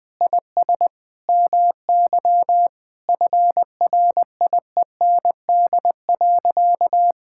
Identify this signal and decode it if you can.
Morse code, heard as a keyed tone, about 20 wpm, 700 Hz. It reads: IS MY FRIEND.
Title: Sentences with 8-Word Phrases - Spoken Beginning - 20wpm